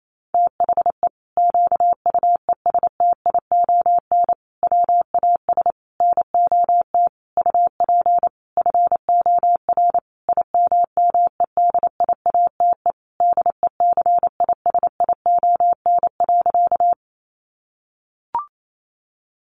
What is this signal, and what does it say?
Morse code, heard as a keyed tone, about 28 wpm, 700 Hz. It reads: THE QUESTION WAS NOT UP FOR IMMEDIATE DECISION.
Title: Sentences from Top 1200 Words 28wpm